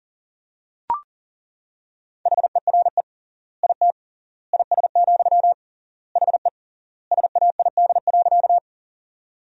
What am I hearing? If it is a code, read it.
Morse code, heard as a keyed tone, about 37 wpm, 700 Hz. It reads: HERE IT IS, HE SAID.
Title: 3x Word-Spacing - Sentences from Top 200 Words - 40wpm